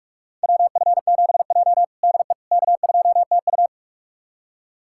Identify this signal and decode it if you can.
Morse code, heard as a keyed tone, about 46 wpm, 700 Hz. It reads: WF7J DE K2TU
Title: USA Callsign DE Callsign - 4 Characters - 45wpm